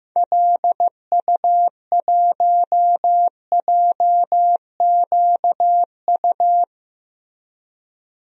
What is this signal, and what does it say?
Morse code, heard as a keyed tone, about 16 wpm, 700 Hz. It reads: LU1JQU
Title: DXCC-100 Call Signs - 6 Characters - Repeated 3x - 15wpm